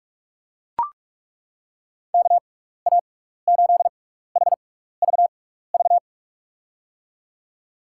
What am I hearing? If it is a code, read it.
Morse code, heard as a keyed tone, about 42 wpm, spaced as for 15 wpm, 700 Hz. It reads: KA8HVV
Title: USA Call Signs - 6 Characters - 6x Character Spacing - 45wpm